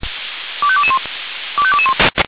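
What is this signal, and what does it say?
Five-tone selective call (ZVEI), heard as quick runs of five tones, tones of about 70 ms. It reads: 25221 25221
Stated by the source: ZVEI five-tone selective call received on SDR.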